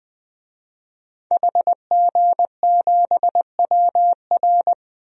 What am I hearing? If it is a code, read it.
Morse code, heard as a keyed tone, about 20 wpm, 700 Hz. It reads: HG7WR
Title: DXCC-100 Call Signs - 5 Characters - Repeated 2x - 20wpm